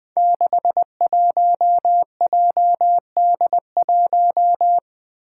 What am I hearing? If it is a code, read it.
Morse code, heard as a keyed tone, about 20 wpm, 700 Hz. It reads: 61JD1
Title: Five Challenging Letter-Number Combinations 20wpm